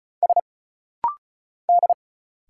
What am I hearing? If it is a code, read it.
Morse code, heard as a keyed tone, about 34 wpm, 700 Hz. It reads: S D
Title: Single Letter-Number Alt - Rapid-Fire 35wpm